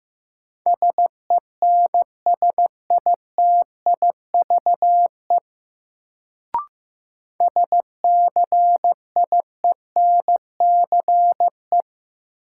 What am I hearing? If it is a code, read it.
Morse code, heard as a keyed tone, about 16 wpm, 700 Hz. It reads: SENSITIVE SCIENCE
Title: Sentences from Top 1400 Words - New Word Drill Alt - 15wpm